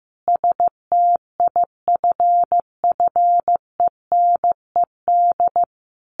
Morse code, heard as a keyed tone, about 15 wpm, 700 Hz. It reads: STIFFENED
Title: A Sound Of Thunder 15wpm